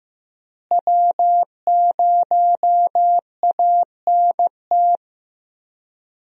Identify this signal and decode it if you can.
Morse code, heard as a keyed tone, about 15 wpm, 700 Hz. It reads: W0ANT
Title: USA Call Signs - 5 Characters - Repeated 2x - 15wpm